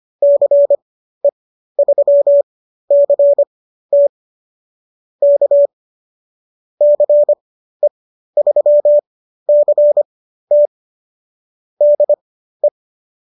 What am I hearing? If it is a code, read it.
Morse code, heard as a keyed tone, about 25 wpm, spaced as for 13 wpm, 590 Hz. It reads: CE3CT K CE3CT DE